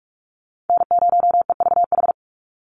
Morse code, heard as a keyed tone, about 44 wpm, 700 Hz. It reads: N0EVH
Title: NA SOTA Chaser Callsigns 45wpm